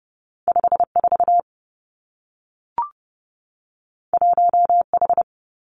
Morse code, heard as a keyed tone, about 30 wpm, 700 Hz. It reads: 5 4 1 H
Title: Two Challenging Letter-Number Combinations 30wpm